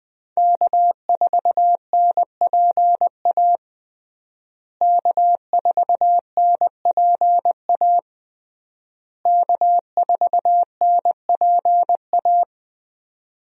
Morse code, heard as a keyed tone, about 20 wpm, 700 Hz. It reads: K4NPA K4NPA K4NPA
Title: USA Call Signs - 5 Characters - Repeated 3x - 20wpm